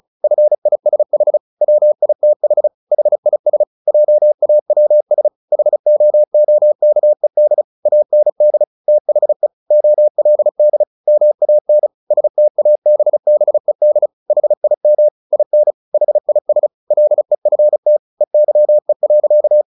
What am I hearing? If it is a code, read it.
Morse code, heard as a keyed tone, about 34 wpm, 600 Hz. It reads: FISH WITH HIS JAWS HOOKED AND THE OLD MAN STABBED HIM IN HIS LEFT EYE.
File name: the old man and the sea 2 35wpm